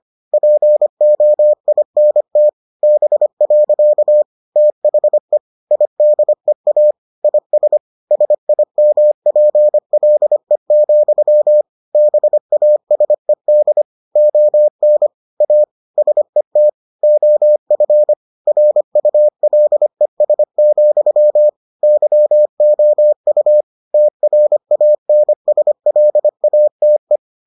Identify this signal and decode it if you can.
Morse code, heard as a keyed tone, about 25 wpm, 600 Hz. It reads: POINT B. THE IDEA IS SIMPLE, BASED ON A SET OF RULES, YOU TRANSLATE